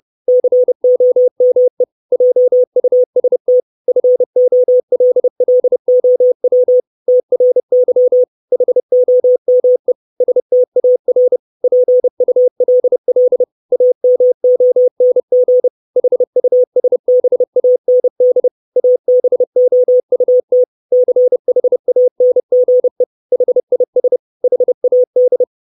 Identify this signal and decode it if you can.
Morse code, heard as a keyed tone, about 30 wpm, 500 Hz. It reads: COME JUST FOLLOW TRY HOME STAR PULL AMONG HUSBAND ABOUT CHANGE HIS HAD
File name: 632 Words 30WPM